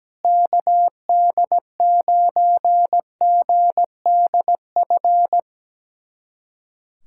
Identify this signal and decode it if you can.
Morse code, heard as a keyed tone, about 17 wpm, 700 Hz. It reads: KD9GDF
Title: USA Callsign DE Callsign 17wpm